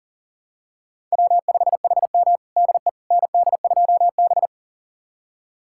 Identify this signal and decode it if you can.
Morse code, heard as a keyed tone, about 40 wpm, 700 Hz. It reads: W5HM DE ND2B
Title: USA Callsign DE Callsign - 4 Characters - 40wpm